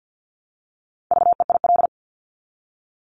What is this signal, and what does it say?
Morse code, heard as a keyed tone, about 50 wpm, 690 Hz. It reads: VEIL
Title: Multi-letter Phonemes Repeated 3x 50wpm